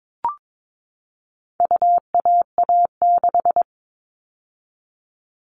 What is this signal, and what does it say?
Morse code, heard as a keyed tone, about 22 wpm, 700 Hz. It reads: UAA6
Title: All Cut Numbers - 4-Digit 22wpm